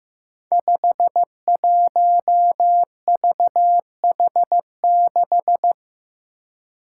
Morse code, heard as a keyed tone, about 15 wpm, 700 Hz. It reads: 51VH6
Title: Five Challenging Letter-Number Combinations 15wpm